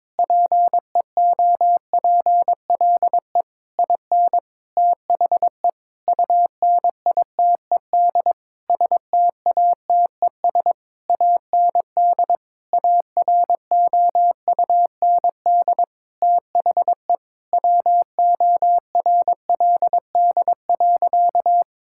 Morse code, heard as a keyed tone, about 22 wpm, 700 Hz. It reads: PEOPLE IN THE UNITED STATES AND AROUND THE WORLD.
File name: practice / 22wpm / Sentences with 8-Word Phrases - 22wpm